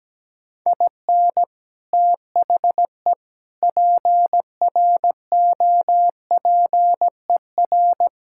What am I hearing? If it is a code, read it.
Morse code, heard as a keyed tone, about 17 wpm, 700 Hz. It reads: IN THE PROPER
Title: Sentences from Top 1000 Words - Review Part 5 - 17wpm